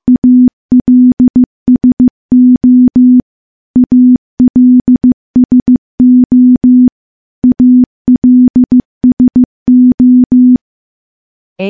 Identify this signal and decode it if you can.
Morse code, heard as a keyed tone, about 15 wpm, 250 Hz. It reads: ALSO ALSO ALSO
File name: morse-training-15WPM